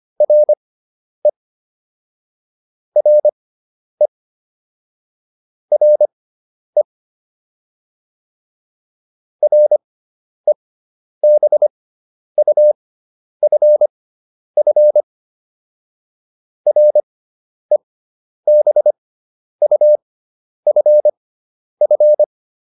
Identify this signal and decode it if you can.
Morse code, heard as a keyed tone, about 25 wpm, spaced as for 10 wpm, 600 Hz. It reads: RE RE RE REBUFF REBUFF
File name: RE6 10